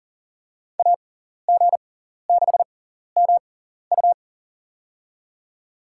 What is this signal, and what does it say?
Morse code, heard as a keyed tone, about 40 wpm, spaced as for 14 wpm, 700 Hz. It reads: AG6MU
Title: USA Call Signs - 6x Character Spacing - 40wpm